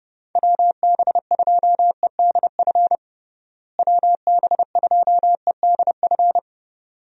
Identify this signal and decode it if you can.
Morse code, heard as a keyed tone, about 30 wpm, 700 Hz. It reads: WB2EDF WB2EDF
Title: USA Call Signs - 6 Characters - Repeated 2x - 30wpm